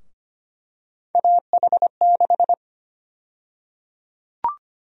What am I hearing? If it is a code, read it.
Morse code, heard as a keyed tone, about 25 wpm, 700 Hz. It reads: AH6
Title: USA Call Sign Prefixes - 25wpm